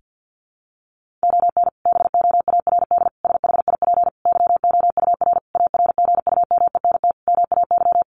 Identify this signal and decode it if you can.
Morse code, heard as a keyed tone, about 50 wpm, 700 Hz. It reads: ON BOARD SHIP YOUR ARGUMENT MAY
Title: Sentences from Top 1400 Words 50wpm